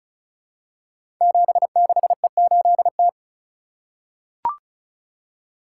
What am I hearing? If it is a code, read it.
Morse code, heard as a keyed tone, about 35 wpm, 700 Hz. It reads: 76E8T
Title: Code Groups - 5 Letter:Numbers - 35wpm